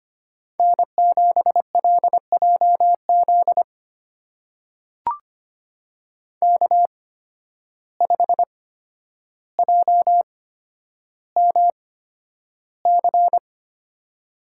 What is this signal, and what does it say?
Morse code, heard as a keyed tone, about 25 wpm, spaced as for 11 wpm, 700 Hz. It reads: N7LJZ K5JMC
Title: USA Call Signs - 5 Characters - 8x Character Spacing - 25wpm